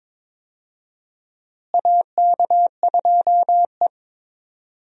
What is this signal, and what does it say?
Morse code, heard as a keyed tone, about 21 wpm, 700 Hz. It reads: AK2E
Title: USA Call Signs - 4 Characters - Repeated 2x - 22wpm